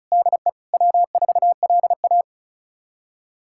Morse code, heard as a keyed tone, about 36 wpm, 700 Hz. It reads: DE W4LA
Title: USA Callsign DE Callsign - 4 Characters - 35wpm